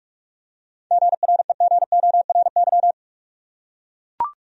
Morse code, heard as a keyed tone, about 44 wpm, 700 Hz. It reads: GREGORY
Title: USA Common Names - 45wpm